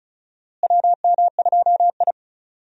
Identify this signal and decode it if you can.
Morse code, heard as a keyed tone, about 36 wpm, 700 Hz. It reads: WM2I
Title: USA Call Signs - 4 Characters - Repeated 2x - 35wpm